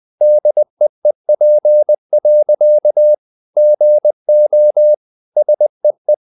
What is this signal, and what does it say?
Morse code, heard as a keyed tone, about 20 wpm, 600 Hz. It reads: DEEP. GO SEE
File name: the old man and the sea part 2 20wpm